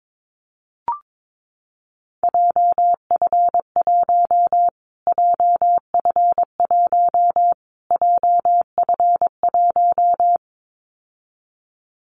Morse code, heard as a keyed tone, about 22 wpm, 700 Hz. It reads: JF1 JF1 JF1
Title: Callsign Prefixes for Top 25 DXCC - Repeated 3x - 22wpm